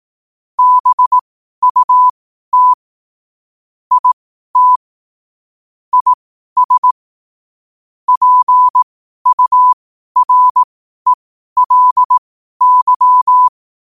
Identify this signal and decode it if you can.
Morse code, heard as a keyed tone, about 18 wpm, spaced as for 12 wpm, 1000 Hz. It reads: BUT IT IS PURELY
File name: qotdfiles / 12wpm